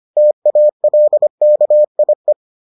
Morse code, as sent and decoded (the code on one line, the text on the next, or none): - .- .-.. -.- .. .
TALKIE